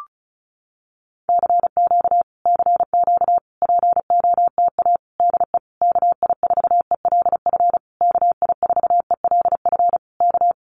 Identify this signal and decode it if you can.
Morse code, heard as a keyed tone, about 34 wpm, 700 Hz. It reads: CQ CQ POTA DE KI4ELF KI4ELF K
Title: Common QSO Phrases Encore 35wpm